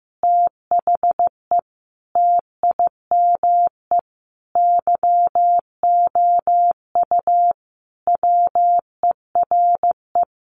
Morse code, heard as a keyed tone, about 15 wpm, 700 Hz. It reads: THE TIME YOU WERE
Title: Sentences from Top 1400 Words 15wpm